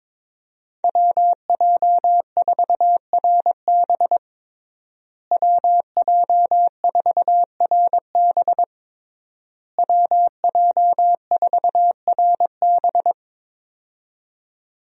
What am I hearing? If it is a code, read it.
Morse code, heard as a keyed tone, about 22 wpm, 700 Hz. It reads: WJ4RB WJ4RB WJ4RB
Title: USA Call Signs - 5 Characters - Repeated 3x - 22wpm